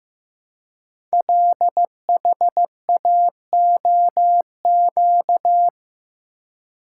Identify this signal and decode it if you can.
Morse code, heard as a keyed tone, about 15 wpm, 700 Hz. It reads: LHAOQ
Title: Code Groups - 5 Letters - 15wpm